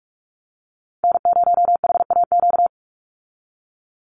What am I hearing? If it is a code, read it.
Morse code, heard as a keyed tone, about 44 wpm, 700 Hz. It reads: N0HAQ
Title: USA Call Signs 45wpm